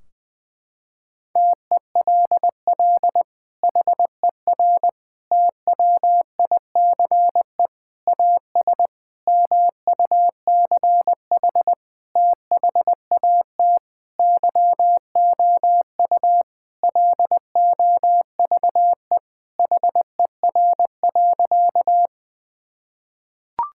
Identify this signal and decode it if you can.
Morse code, heard as a keyed tone, about 20 wpm, 700 Hz. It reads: TELL HER TWICE AS MUCH THAT YOU LOVE HER.
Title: Sentences from Top 1600 Words 20wpm